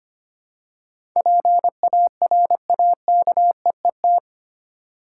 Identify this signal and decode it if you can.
Morse code, heard as a keyed tone, about 25 wpm, 700 Hz. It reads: PARAKEET
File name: practice / 25wpm / Difficult Words with Hints - 25wpm